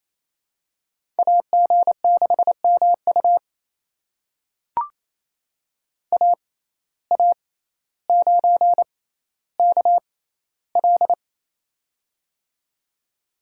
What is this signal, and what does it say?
Morse code, heard as a keyed tone, about 27 wpm, spaced as for 14 wpm, 700 Hz. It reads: AG6MU AA9KL
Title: USA Call Signs - 5 Characters - 6x Character Spacing - 28wpm